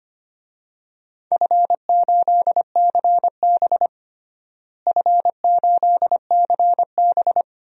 Morse code, heard as a keyed tone, about 25 wpm, 700 Hz. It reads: F8CB F8CB
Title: DXCC-100 Call Signs - 4 Characters - Repeated 2x - 25wpm